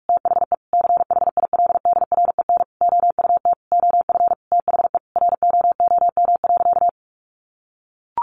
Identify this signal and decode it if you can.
Morse code, heard as a keyed tone, about 44 wpm, 700 Hz. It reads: THE CHILDREN OUT OF THE ROOM.
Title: Sentences from Top 1500 Words 45wpm